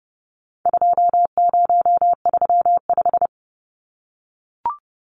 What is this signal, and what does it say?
Morse code, heard as a keyed tone, about 30 wpm, 700 Hz. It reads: 2035
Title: Numbers - 4-Digit 30wpm